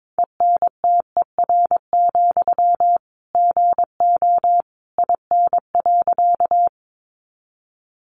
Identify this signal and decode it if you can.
Morse code, heard as a keyed tone, about 22 wpm, 700 Hz. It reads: ENTER, GO IN.
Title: Sentences from Top 1300 Words 22wpm